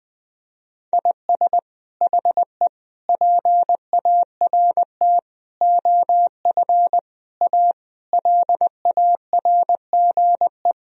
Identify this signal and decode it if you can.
Morse code, heard as a keyed tone, about 21 wpm, 700 Hz. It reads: IS HE PART OF A LARGE
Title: Sentences from Top 1000 Words - Review Part 4 - 20wpm